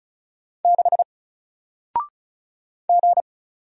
Morse code, heard as a keyed tone, about 36 wpm, 700 Hz. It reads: 6 G
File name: Single-Letter-Number-35wpm